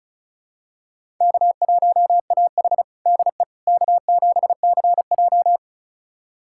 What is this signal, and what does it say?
Morse code, heard as a keyed tone, about 36 wpm, 700 Hz. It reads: K1AH DE K7CJ
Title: USA Callsign DE Callsign - 4 Characters - 35wpm